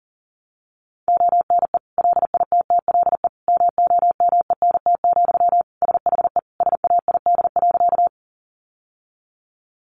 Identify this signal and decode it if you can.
Morse code, heard as a keyed tone, about 40 wpm, 700 Hz. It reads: ONE LITTLE MOMENT, SHE SAID.
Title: Sentences from Top 1000 Words 40wpm